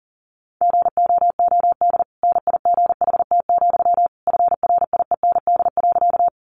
Morse code, heard as a keyed tone, about 40 wpm, 700 Hz. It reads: GOOD NIGHT, FRIEND.
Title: Sentences from Top 300 Words 40wpm